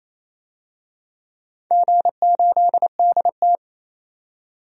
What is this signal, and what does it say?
Morse code, heard as a keyed tone, about 28 wpm, 690 Hz. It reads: G8DT
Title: DXCC-100 Call Signs - 4 Characters - 28wpm